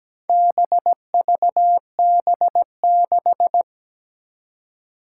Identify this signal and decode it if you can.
Morse code, heard as a keyed tone, about 17 wpm, 700 Hz. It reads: BVB6
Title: Four Challenging Letter-Number Combinations 17wpm